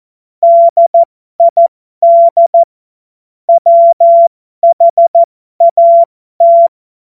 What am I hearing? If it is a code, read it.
Morse code, heard as a keyed tone, about 14 wpm, spaced as for 12 wpm, 680 Hz. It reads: DID WHAT
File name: Morsekurs / Witze